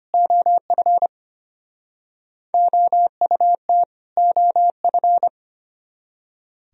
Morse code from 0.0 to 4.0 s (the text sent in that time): OF OUT